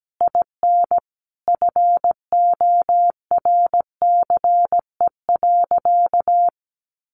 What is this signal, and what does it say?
Morse code, heard as a keyed tone, about 17 wpm, 700 Hz. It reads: IN FORCE.
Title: Sentences from Top 700 Words 17wpm